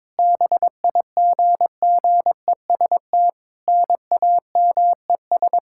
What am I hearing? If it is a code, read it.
Morse code, heard as a keyed tone, about 22 wpm, 700 Hz. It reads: BIGGEST NAMES
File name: Sentences with 7-Word Phrases - 22wpm